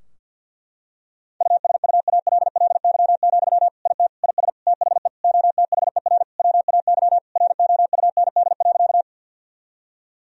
Morse code, heard as a keyed tone, about 50 wpm, 700 Hz. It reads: USUALLY, IT IS THE OTHER WAY ROUND.
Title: Sentences from Top 1000 Words - Review Part 3 - 50wpm